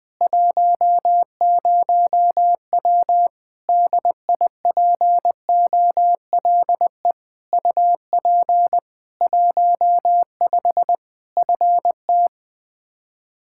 Morse code, heard as a keyed tone, about 20 wpm, 700 Hz. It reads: 10W DIPOLE UP 15 FT
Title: Practice QSO - Exchange 2 - Info - 20wpm